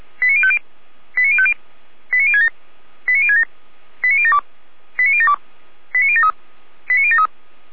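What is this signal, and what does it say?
Five-tone selective call (ZVEI), heard as quick runs of five tones, tones of about 70 ms. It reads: 89050 89050 89067 89067 89072 89072 89073 89073
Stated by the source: ZVEI five-tone selective call received by SDR, fire brigade Feuerwehr Orlamünde (F85360).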